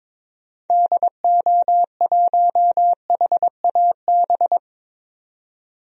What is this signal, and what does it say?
Morse code, heard as a keyed tone, about 22 wpm, 700 Hz. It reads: DO1HAB
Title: DXCC-100 Call Signs - 6 Characters - 22wpm